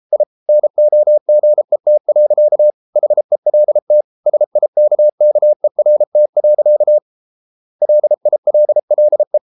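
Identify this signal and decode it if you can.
Morse code, heard as a keyed tone, about 33 wpm, 600 Hz. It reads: I NOGET. HELT SIKKERT. LILLE
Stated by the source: Morse code at 33 wpm, QRQ.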